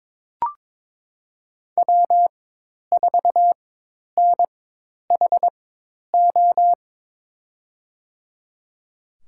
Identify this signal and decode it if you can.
Morse code, heard as a keyed tone, about 22 wpm, spaced as for 10 wpm, 700 Hz. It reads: W4NHO
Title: USA Call Signs - 5 Characters - 4x Character Spacing - 22wpm